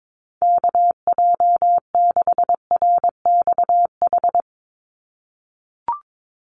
Morse code, heard as a keyed tone, about 22 wpm, 700 Hz. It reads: KJ6RXH
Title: USA Call Signs 22wpm